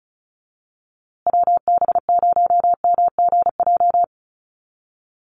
Morse code, heard as a keyed tone, about 34 wpm, 700 Hz. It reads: WB0MGJ
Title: USA Call Signs Repeated 2x 35wpm